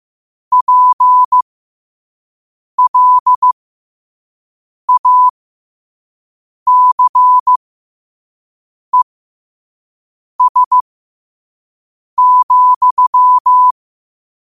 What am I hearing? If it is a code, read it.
Morse code, heard as a keyed tone, about 15 wpm, spaced as for 5 wpm, 1000 Hz. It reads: PLACES,
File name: qotdfiles / 5wpm